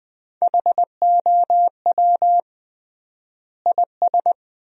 Morse code, heard as a keyed tone, about 21 wpm, 700 Hz. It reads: HOW IS
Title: 3x Word-Spacing - Sentences from Top 100 Words - 20wpm